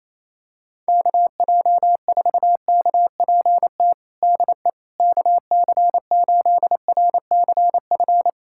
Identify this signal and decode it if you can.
Morse code, heard as a keyed tone, about 28 wpm, 700 Hz. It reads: KJ4KPT DE KC8RCF
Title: USA Callsign DE Callsign - 6 Characters - 28wpm